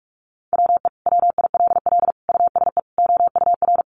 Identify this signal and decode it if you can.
Morse code, heard as a keyed tone, about 44 wpm, 700 Hz. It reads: WE WILL USE OUR
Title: Sentences from Top 1100 Words 45wpm